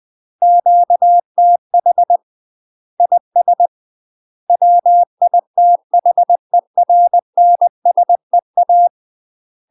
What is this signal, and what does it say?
Morse code, heard as a keyed tone, about 20 wpm, 700 Hz. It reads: QTH IS WITHERNSEA
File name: UK MASTER QSOs TRAINING/20WPM